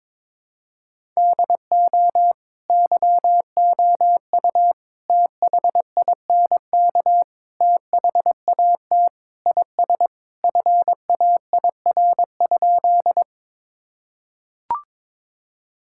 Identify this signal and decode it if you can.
Morse code, heard as a keyed tone, about 22 wpm, 700 Hz. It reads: DO YOU THINK THAT IS FAIR?
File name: practice / 22wpm / Sentences with 4-Word Phrases - 22wpm